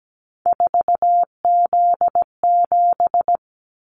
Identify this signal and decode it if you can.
Morse code, heard as a keyed tone, about 17 wpm, 700 Hz. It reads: 4Z7
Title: Callsign Prefixes for Top 50 DXCC - 17wpm